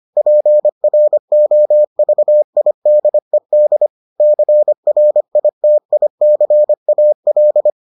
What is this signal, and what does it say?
Morse code, heard as a keyed tone, about 25 wpm, 600 Hz. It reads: PROVIDED CRITICAL